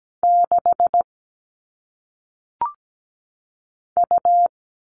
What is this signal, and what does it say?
Morse code, heard as a keyed tone, about 17 wpm, 700 Hz. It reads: 6 U
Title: All Cut Numbers - 1-Digit 17wpm